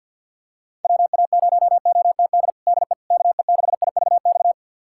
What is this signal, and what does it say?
Morse code, heard as a keyed tone, about 47 wpm, 700 Hz. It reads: WA0OTD DE KE6IVX
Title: USA Callsign DE Callsign 50wpm